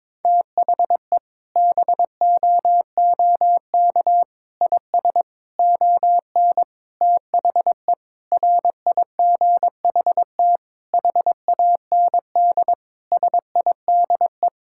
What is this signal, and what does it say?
Morse code, heard as a keyed tone, about 22 wpm, 700 Hz. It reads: THE BOOK IS ON THE RIGHT HAND SIDE
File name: Sentences with 7-Word Phrases - 22wpm